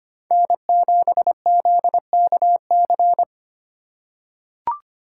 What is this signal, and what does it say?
Morse code, heard as a keyed tone, about 25 wpm, 700 Hz. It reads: N7ZKC
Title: USA Call Signs - 5 Characters - 3x Character Spacing - 25wpm